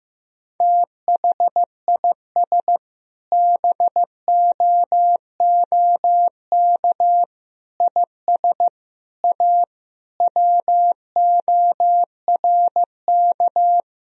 Morse code, heard as a keyed tone, about 15 wpm, 700 Hz. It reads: THIS BOOK IS A WORK
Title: Sentences from Top 1000 Words - Review Part 3 - 15wpm